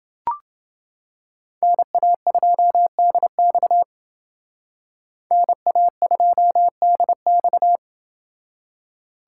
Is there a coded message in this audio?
Yes: Morse code from 1.6 to 7.8 s, heard as a keyed tone, about 28 wpm, 700 Hz. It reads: NA2DX NA2DX